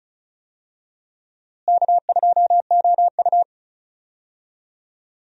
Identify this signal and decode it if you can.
Morse code, heard as a keyed tone, about 34 wpm, 700 Hz. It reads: K2OU
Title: USA Call Signs - 4 Characters - Repeated 2x - 35wpm